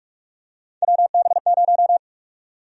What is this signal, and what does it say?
Morse code, heard as a keyed tone, about 42 wpm, 700 Hz. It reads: WB0
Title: USA Call Sign Prefixes - 45wpm